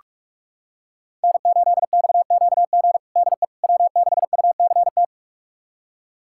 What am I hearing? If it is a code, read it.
Morse code, heard as a keyed tone, about 46 wpm, 700 Hz. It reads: N8XQG DE W6UCT
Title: USA Callsign DE Callsign - 5 Characters - 45wpm